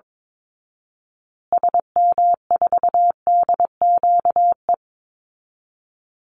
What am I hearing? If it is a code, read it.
Morse code, heard as a keyed tone, about 22 wpm, 690 Hz. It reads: SM4DQE
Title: Call Signs Repeated 3x - 22wpm